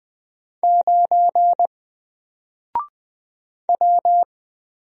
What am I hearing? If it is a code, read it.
Morse code, heard as a keyed tone, about 20 wpm, 700 Hz. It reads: 9 W
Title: Single Letter-Number Alt 20wpm